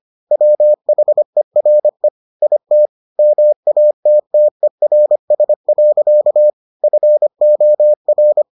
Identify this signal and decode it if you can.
Morse code, heard as a keyed tone, about 25 wpm, 600 Hz. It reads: WHERE IT MATTERS. FOR